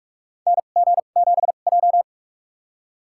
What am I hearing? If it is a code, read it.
Morse code, heard as a keyed tone, about 45 wpm, spaced as for 32 wpm, 700 Hz. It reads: NG7J
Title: USA Call Signs - 2x Character Spacing - 45wpm